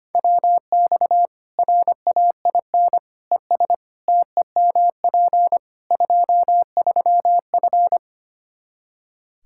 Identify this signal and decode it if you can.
Morse code, heard as a keyed tone, about 25 wpm, 700 Hz. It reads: WX RAIN ES TEMP 23F
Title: K1USN SST QSO Practice 25wpm